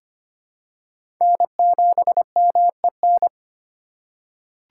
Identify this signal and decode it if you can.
Morse code, heard as a keyed tone, about 25 wpm, 700 Hz. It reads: N7MEN
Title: USA Call Signs - 5 Characters - 3x Character Spacing - 25wpm